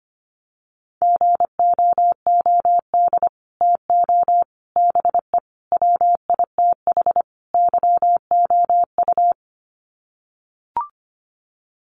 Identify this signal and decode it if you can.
Morse code, heard as a keyed tone, about 25 wpm, 700 Hz. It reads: GOOD TO BE WITH YOU
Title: Sets of 5 Words Encore2 25wpm